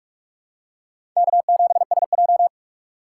Morse code, heard as a keyed tone, about 45 wpm, 700 Hz. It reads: K7SJ
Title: USA Call Signs - 4 Characters - 3x Character Spacing - 45wpm